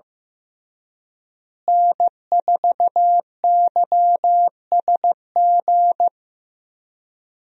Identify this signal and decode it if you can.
Morse code, heard as a keyed tone, about 15 wpm, 700 Hz. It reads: N4YSG
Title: USA Call Signs - 5 Characters - Repeated 3x - 15wpm